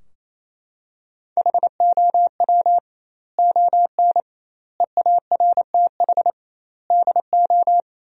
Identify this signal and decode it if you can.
Morse code, heard as a keyed tone, about 28 wpm, 700 Hz. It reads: HOW ON EARTH DO
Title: 2x Word-Spacing - Sentences from Top 200 Words - 28wpm